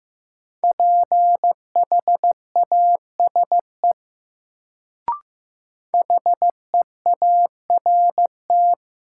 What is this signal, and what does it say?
Morse code, heard as a keyed tone, about 15 wpm, 700 Hz. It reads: PHASE HEART
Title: Sentences from Top 1200 Words - New Word Drill Alt - 15wpm